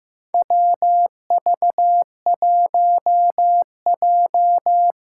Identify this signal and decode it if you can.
Morse code, heard as a keyed tone, about 15 wpm, 700 Hz. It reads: WV1J
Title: USA Callsign DE Callsign - 4 Characters - 15wpm